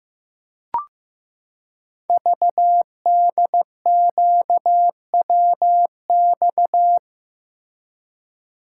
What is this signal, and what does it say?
Morse code, heard as a keyed tone, about 15 wpm, 700 Hz. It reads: VDQWX
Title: Code Groups - 5 Letter:Numbers - 15wpm